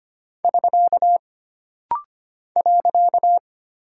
Morse code, heard as a keyed tone, about 25 wpm, 700 Hz. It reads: <SK> .
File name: Single Punctuation-Prosign - Rapid-Fire - 25wpm